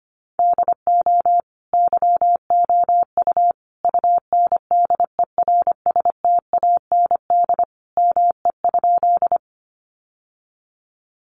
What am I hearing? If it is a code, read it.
Morse code, heard as a keyed tone, about 25 wpm, 700 Hz. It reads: DO YOU UNDERSTAND ME?
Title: Sentences from Top 600 Words 25wpm